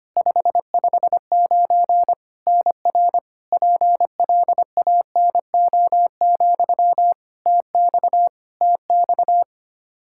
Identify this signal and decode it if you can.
Morse code, heard as a keyed tone, about 25 wpm, 700 Hz. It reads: 559 NR PLANO, TX TX
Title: Practice QSO - Exchange 1 - Beginning - 25wpm